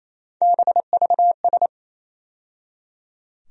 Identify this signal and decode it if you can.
Morse code, heard as a keyed tone, about 28 wpm, 700 Hz. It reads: BVS
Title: USA Call Sign Postfixes - 3 Letters 28wpm